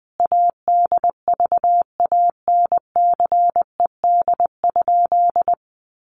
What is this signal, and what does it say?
Morse code, heard as a keyed tone, about 20 wpm, 700 Hz. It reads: ADVANCED?
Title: Sentences from Top 1200 Words 20wpm